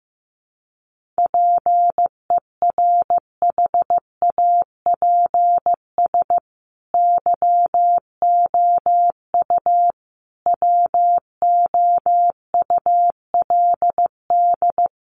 Morse code, heard as a keyed tone, about 15 wpm, 700 Hz. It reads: PERHAPS YOU WOULD
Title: Sentences from Top 1100 Words 15wpm